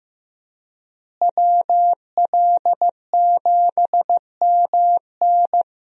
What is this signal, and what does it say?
Morse code, heard as a keyed tone, about 15 wpm, 700 Hz. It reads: WL7MN
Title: USA Call Signs - 5 Characters - Repeated 3x - 15wpm